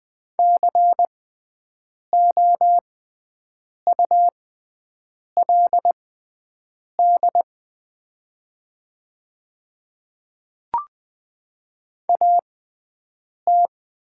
Morse code, heard as a keyed tone, about 20 wpm, spaced as for 7 wpm, 690 Hz. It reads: COULD AT
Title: 6x Character Spacing - Top 500 Words - 20wpm